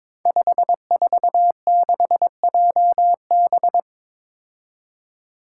Morse code, heard as a keyed tone, about 22 wpm, 700 Hz. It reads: 546JB
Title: Five Challenging Letter-Number Combinations 22wpm